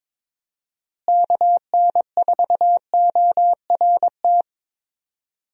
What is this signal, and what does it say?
Morse code, heard as a keyed tone, about 22 wpm, 700 Hz. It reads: KN4ORT
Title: USA Call Signs - 6 Characters - Repeated 2x - 22wpm